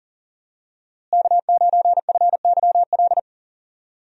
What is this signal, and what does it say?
Morse code, heard as a keyed tone, about 37 wpm, 700 Hz. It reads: K9FYL
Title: USA Call Signs - 5 Characters - Repeated 2x - 40wpm